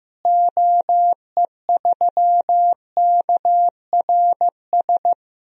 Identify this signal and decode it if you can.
Morse code, heard as a keyed tone, about 15 wpm, 700 Hz. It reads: OE3KRS
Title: DXCC-100 Call Signs - 6 Characters - Repeated 2x - 15wpm